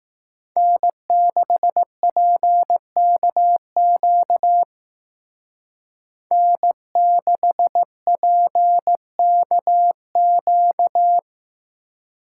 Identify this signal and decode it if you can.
Morse code, heard as a keyed tone, about 18 wpm, 700 Hz. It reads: N6PKQ N6PKQ
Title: Speed Racing - USA 1x3 Call Signs 15wpm